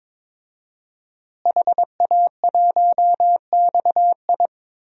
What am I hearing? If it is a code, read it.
Morse code, heard as a keyed tone, about 22 wpm, 700 Hz. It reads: HA1XI
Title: DXCC-100 Call Signs - 5 Characters - Repeated 3x - 22wpm